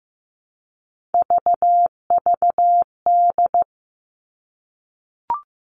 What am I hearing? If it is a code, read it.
Morse code, heard as a keyed tone, about 15 wpm, 700 Hz. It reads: VVD
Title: Three Challenging Letter-Number Combinations 15wpm